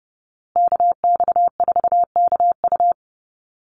Morse code, heard as a keyed tone, about 30 wpm, 700 Hz. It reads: KX4KU
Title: Call Signs Repeated 2x - 30wpm